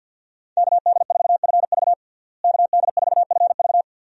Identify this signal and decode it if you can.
Morse code, heard as a keyed tone, about 52 wpm, 700 Hz. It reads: KD4FV KD4FV
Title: USA Call Signs - 5 Characters - Repeated 2x - 50wpm